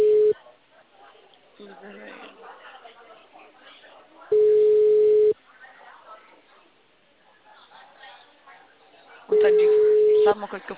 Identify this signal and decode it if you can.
Ringback tone (Europe, 425 Hz), with speech.